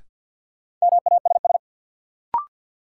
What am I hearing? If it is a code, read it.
Morse code, heard as a keyed tone, about 53 wpm, 700 Hz. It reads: MASS
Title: Sets of 5 Words Encore - New Word Drill - 50wpm